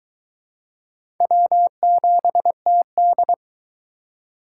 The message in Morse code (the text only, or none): W7TD